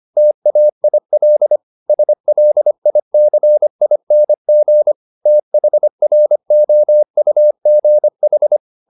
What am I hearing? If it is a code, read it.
Morse code, heard as a keyed tone, about 25 wpm, 600 Hz. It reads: TAIL SLICING THROUGH
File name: the old man and the sea part 1 25wpm